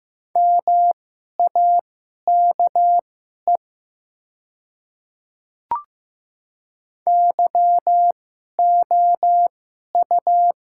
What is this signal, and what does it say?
Morse code, heard as a keyed tone, about 15 wpm, spaced as for 11 wpm, 700 Hz. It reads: MAKE YOU
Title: 2x Character Spacing - Top 100 Words - 15wpm